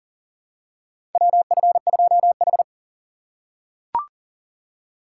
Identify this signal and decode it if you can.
Morse code, heard as a keyed tone, about 40 wpm, 700 Hz. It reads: WF2H
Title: USA Call Signs - 4 Characters - 4x Character Spacing - 40wpm